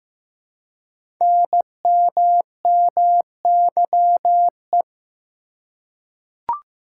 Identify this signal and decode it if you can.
Morse code, heard as a keyed tone, about 15 wpm, 700 Hz. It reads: NMMYE
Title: Code Groups - 5 Letters - 15wpm